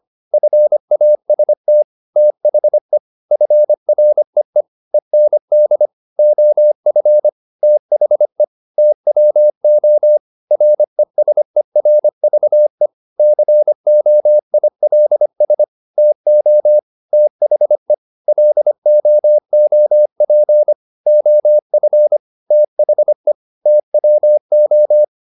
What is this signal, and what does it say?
Morse code, heard as a keyed tone, about 25 wpm, 600 Hz. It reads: FAST THE FREE END OF THE TWO RESERVE COILS TO THE LOOP OF THE TWO
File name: the old man and the sea part 1 25wpm